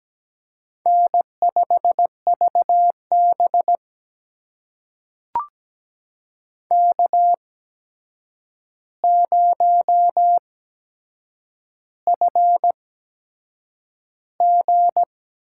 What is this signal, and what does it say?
Morse code, heard as a keyed tone, about 17 wpm, spaced as for 7 wpm, 700 Hz. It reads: N5VB K0FG
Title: USA Call Signs - 4 Characters - 8x Character Spacing - 17wpm